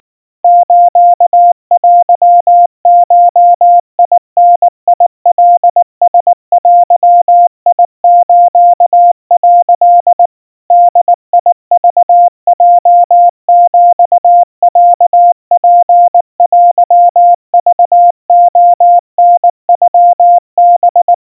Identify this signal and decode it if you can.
Morse code (Japanese Wabun code), heard as a keyed tone, about 19 wpm, 700 Hz. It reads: ステゴダカラデス」 ボクヲヒロツテクレタノハ
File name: kawa / Morse sample / kamyu 70 cpm 700 Hz Crispy